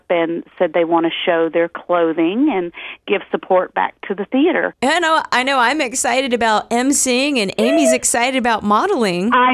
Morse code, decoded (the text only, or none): none